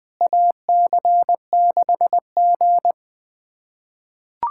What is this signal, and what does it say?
Morse code, heard as a keyed tone, about 21 wpm, 700 Hz. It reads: AC6G
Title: USA Callsign DE Callsign - 4 Characters - 20wpm